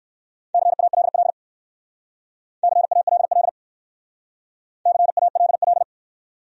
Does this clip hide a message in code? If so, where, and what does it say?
Yes: Morse code from 0.5 to 5.8 s, heard as a keyed tone, about 59 wpm, 690 Hz. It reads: CALL CALL CALL